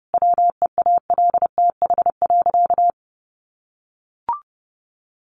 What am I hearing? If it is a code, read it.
Morse code, heard as a keyed tone, about 30 wpm, 700 Hz. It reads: WEALTH.
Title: Sentences from Top 1500 Words 30wpm